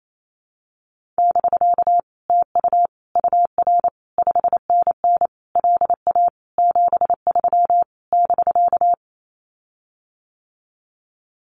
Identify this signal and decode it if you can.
Morse code, heard as a keyed tone, about 28 wpm, 700 Hz. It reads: <BK> TU UR 5NN LA 73 <BK>
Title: Common QSO Phrases Encore 28wpm